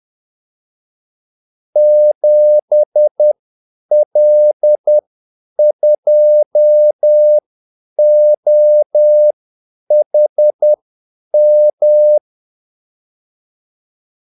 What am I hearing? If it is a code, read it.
Morse code, heard as a keyed tone, about 10 wpm, spaced as for 8 wpm, 600 Hz. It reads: 7L2OHM
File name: Callsigns-10-8wpm